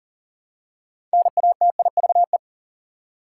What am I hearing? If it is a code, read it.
Morse code, heard as a keyed tone, about 40 wpm, 700 Hz. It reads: NATIVE
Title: Six Letter Words - 40wpm